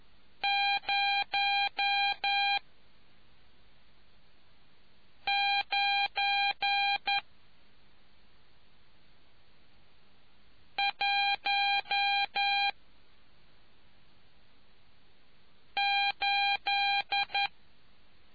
Morse code, heard as a keyed tone, about 11 wpm, 800 Hz. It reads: 0 9 1 8